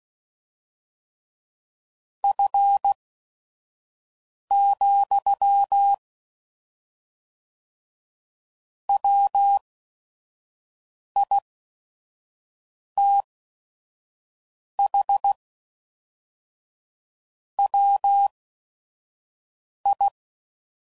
Morse code, heard as a keyed tone, about 16 wpm, spaced as for 5 wpm, 800 Hz. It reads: F, WITH WI
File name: CW Code Groups @5WPM / Weather Sets / Weather 4 10